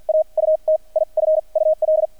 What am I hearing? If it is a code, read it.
Morse code, heard as a keyed tone, about 50 wpm, spaced as for 36 wpm, 630 Hz. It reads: AUTIFUL
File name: archived / CRSnet